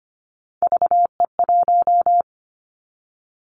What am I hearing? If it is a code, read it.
Morse code, heard as a keyed tone, about 25 wpm, 700 Hz. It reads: VE1
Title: Callsign Prefixes for Top 25 DXCC - Repeated 3x - 25wpm